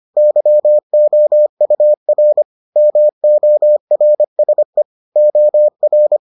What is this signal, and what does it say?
Morse code, heard as a keyed tone, about 25 wpm, 600 Hz. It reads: YOUR MORSE OR